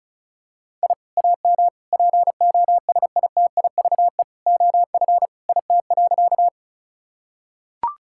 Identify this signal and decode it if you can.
Morse code, heard as a keyed tone, about 34 wpm, 700 Hz. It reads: I AM POSITIVE OF IT.
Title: Sentences from Top 1000 Words - Review Part 5 - 35wpm